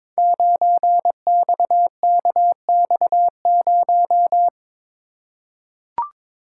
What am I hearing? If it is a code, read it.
Morse code, heard as a keyed tone, about 22 wpm, 700 Hz. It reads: 9XKX0
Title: Code Groups - 5 Letter:Numbers - 22wpm